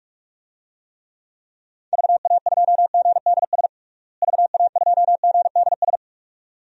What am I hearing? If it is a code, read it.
Morse code, heard as a keyed tone, about 46 wpm, 700 Hz. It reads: VA2GDS VA2GDS
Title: DXCC-100 Call Signs - 6 Characters - Repeated 2x - 45wpm